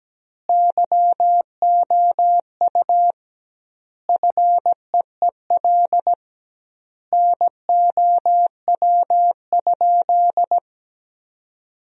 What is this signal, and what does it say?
Morse code, heard as a keyed tone, about 17 wpm, 700 Hz. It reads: YOU FEEL NOW?
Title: 2x Word-Spacing - Sentences from Top 400 Words - 17wpm